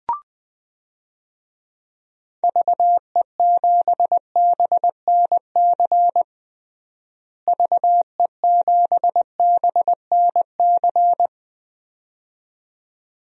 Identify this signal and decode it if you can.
Morse code, heard as a keyed tone, about 21 wpm, 700 Hz. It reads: VE7BNC VE7BNC
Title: DXCC-100 Call Signs - 6 Characters - Repeated 2x - 20wpm